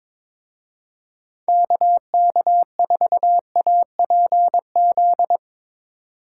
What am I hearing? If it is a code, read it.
Morse code, heard as a keyed tone, about 24 wpm, 700 Hz. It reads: KK4APZ
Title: Speed Racing - USA 2x3 Call Signs 22wpm